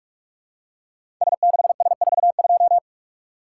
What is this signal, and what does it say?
Morse code, heard as a keyed tone, about 44 wpm, 700 Hz. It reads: S6S42
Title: Code Groups - 5 Letter:Numbers - 45wpm